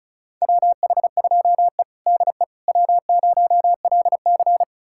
Morse code, heard as a keyed tone, about 34 wpm, 700 Hz. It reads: WH2E DE W0LC
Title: USA Callsign DE Callsign - 4 Characters - 35wpm